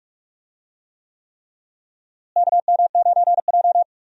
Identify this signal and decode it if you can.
Morse code, heard as a keyed tone, about 44 wpm, 700 Hz. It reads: KM9J
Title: USA Call Signs - 4 Characters - Repeated 3x - 45wpm